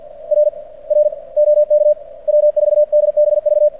Morse code, heard as a keyed tone, about 46 wpm, 590 Hz. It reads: R R OK W4DBV
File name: archived / CRSnet